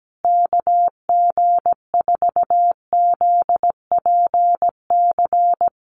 Morse code, heard as a keyed tone, about 17 wpm, 700 Hz. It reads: KG4ZPC
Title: USA Call Signs Repeated 2x 17wpm